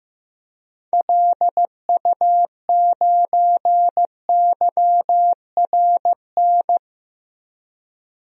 Morse code, heard as a keyed tone, about 15 wpm, 700 Hz. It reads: LU9YRN
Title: DXCC-100 Call Signs - 6 Characters - 15wpm